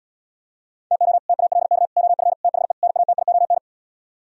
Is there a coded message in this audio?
Yes: Morse code from 0.9 to 3.6 s, heard as a keyed tone, about 77 wpm, 700 Hz. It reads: TO TALK OF THE NATION